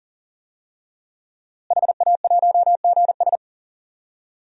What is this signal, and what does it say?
Morse code, heard as a keyed tone, about 40 wpm, 700 Hz. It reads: HA1GS
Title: DXCC-100 Call Signs - 5 Characters - Repeated 2x - 40wpm